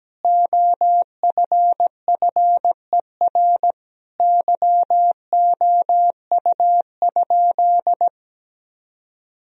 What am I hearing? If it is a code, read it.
Morse code, heard as a keyed tone, about 17 wpm, 700 Hz. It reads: OFFER YOU?
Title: Sentences from Top 1000 Words - Review Part 4 - 17wpm